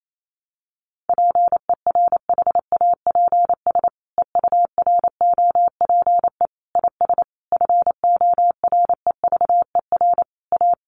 Morse code, heard as a keyed tone, about 28 wpm, 700 Hz. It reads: PERHAPS EUROPE IS FOREVER A
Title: A Sound Of Thunder 28wpm